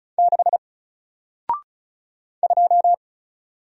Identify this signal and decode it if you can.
Morse code, heard as a keyed tone, about 36 wpm, 700 Hz. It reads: 6 2
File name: Single-Letter-Number-35wpm